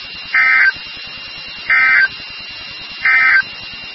Emergency Alert System (SAME) tones, heard as harsh buzzing bursts. It is heard from 0.2 to 3.5 s.
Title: NOAA Weather Radio WXK36